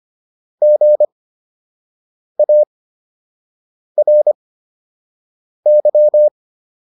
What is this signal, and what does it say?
Morse code, heard as a keyed tone, about 25 wpm, spaced as for 6 wpm, 600 Hz. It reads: GARY